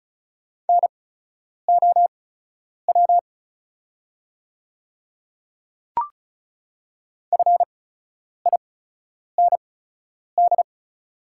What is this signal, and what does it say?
Morse code, heard as a keyed tone, about 36 wpm, spaced as for 10 wpm, 690 Hz. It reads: NOW FIND
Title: 8x Character Spacing - Top 200 Words - 35wpm